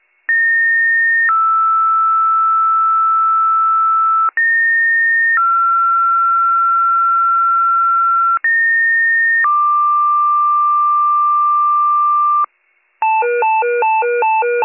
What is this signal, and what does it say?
Two-tone page, heard as a one-second tone, then a three-second tone, about 1850 then 1320 Hz, about 1850 then 1360 Hz, about 1850 then 1180 Hz, with noise. It is heard from 0.3 to 12.5 s.